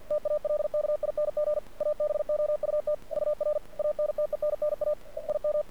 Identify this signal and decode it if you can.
Morse code, heard as a keyed tone, about 52 wpm, 600 Hz. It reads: TALKING ABOUT UR ANTENNA IM
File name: archived / CRSnet